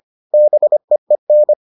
Morse code, heard as a keyed tone, about 25 wpm, 600 Hz. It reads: BEEN